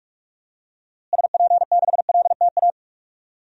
Morse code, heard as a keyed tone, about 46 wpm, 700 Hz. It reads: SP6LTA
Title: DXCC-100 Call Signs - 6 Characters - Repeated 3x - 45wpm